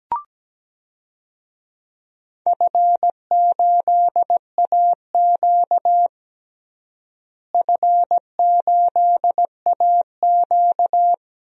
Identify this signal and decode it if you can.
Morse code, heard as a keyed tone, about 17 wpm, 700 Hz. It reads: F8AQ F8AQ
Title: DXCC-100 Call Signs - 4 Characters - Repeated 2x - 17wpm